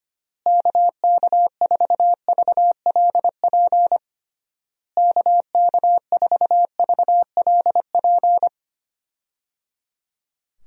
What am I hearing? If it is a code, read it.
Morse code, heard as a keyed tone, about 25 wpm, 700 Hz. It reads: KK4VLP KK4VLP
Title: USA Call Signs - 6 Characters - Repeated 3x - 25wpm